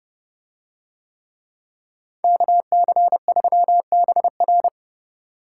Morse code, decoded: KC3BR

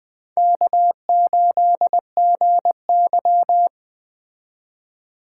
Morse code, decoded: K8GY

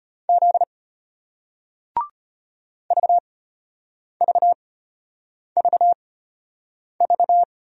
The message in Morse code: Z VVVV

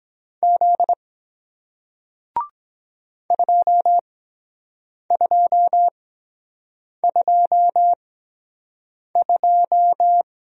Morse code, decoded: Z 2222

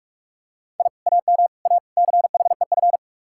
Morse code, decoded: I AM A CHEF

